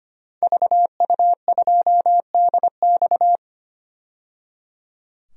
Morse code, heard as a keyed tone, about 25 wpm, 700 Hz. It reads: VU2DX
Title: DXCC-100 Call Signs - 5 Characters - 25wpm